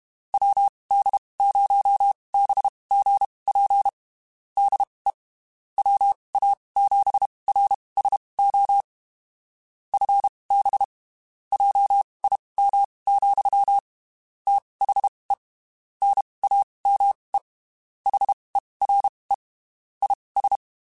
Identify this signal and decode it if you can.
Morse code, heard as a keyed tone, about 32 wpm, spaced as for 23 wpm, 800 Hz. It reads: WD0BGP DE WA7RSO FB JIM, THE NAME HERE IS
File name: (spaced) / QSOs Spaced-mp3s / WD0BGP 1 @30 sp